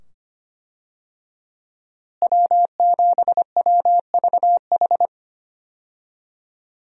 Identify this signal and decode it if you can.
Morse code, heard as a keyed tone, about 25 wpm, 700 Hz. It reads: W7WVH
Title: USA Call Signs - 5 Characters - Repeated 2x - 25wpm